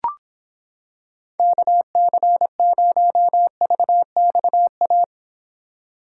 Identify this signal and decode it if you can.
Morse code, heard as a keyed tone, about 26 wpm, 700 Hz. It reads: KC0VXA